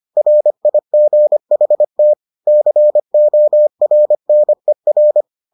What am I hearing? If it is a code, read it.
Morse code, heard as a keyed tone, about 25 wpm, 600 Hz. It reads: RIGHT CORNER